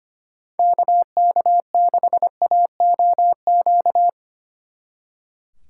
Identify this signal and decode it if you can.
Morse code, heard as a keyed tone, about 31 wpm, 700 Hz. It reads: KK6AOQ
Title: Speed Racing - USA 2x3 Call Signs 25wpm